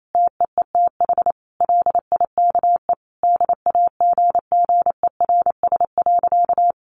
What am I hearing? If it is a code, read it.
Morse code, heard as a keyed tone, about 28 wpm, 700 Hz. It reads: TEETH LIKE DAGGERS.
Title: A Sound Of Thunder 28wpm